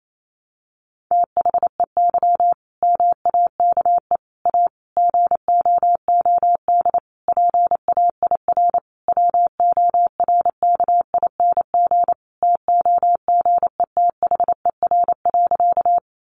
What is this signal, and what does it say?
Morse code, heard as a keyed tone, about 28 wpm, 700 Hz. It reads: THEY MAKE A GOOD PAIR WORKING TOGETHER.
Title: Sentences from Top 1200 Words 28wpm